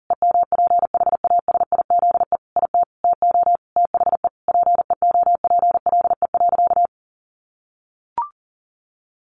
Morse code, heard as a keyed tone, about 40 wpm, 700 Hz. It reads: EMPHASIZE IT TO THE PEOPLE.
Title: Sentences from Top 1500 Words 40wpm